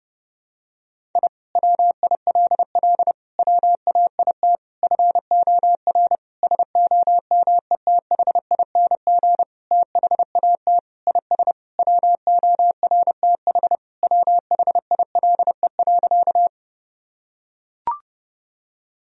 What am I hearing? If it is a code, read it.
Morse code, heard as a keyed tone, about 30 wpm, 700 Hz. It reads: I WILL WAIT FOR SOMETHING THAT IS WORTH WHILE.
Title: Sentences from Top 1000 Words - Review Part 1 - 30wpm